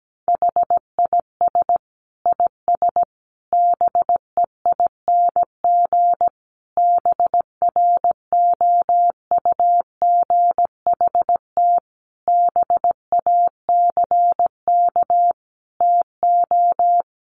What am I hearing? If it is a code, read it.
Morse code, heard as a keyed tone, about 17 wpm, 700 Hz. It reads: HIS IS BEING BROUGHT BACK TO
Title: Sentences from Top 600 Words 17wpm